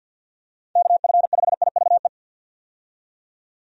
Morse code, heard as a keyed tone, about 48 wpm, 700 Hz. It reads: KF5IVE
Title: USA Call Signs - 6 Characters - 50wpm